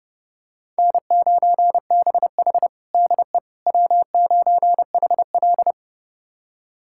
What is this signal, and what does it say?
Morse code, heard as a keyed tone, about 30 wpm, 700 Hz. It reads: N9BH DE W9HL
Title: USA Callsign DE Callsign - 4 Characters - 30wpm